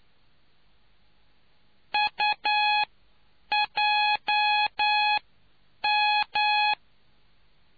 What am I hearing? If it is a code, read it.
Morse code, heard as a keyed tone, about 9 wpm, spaced as for 7 wpm, 840 Hz. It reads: UJM